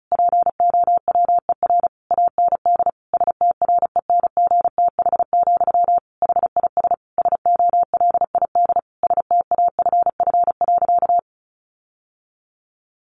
Morse code, heard as a keyed tone, about 34 wpm, 700 Hz. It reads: POWER AND STRENGTH, HIS SOLID STAFF.
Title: Sentences from Top 1500 Words 35wpm